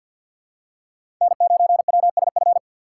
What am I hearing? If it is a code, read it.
Morse code, heard as a keyed tone, about 53 wpm, 700 Hz. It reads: N9WSF
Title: USA Call Signs - 6x Character Spacing - 50wpm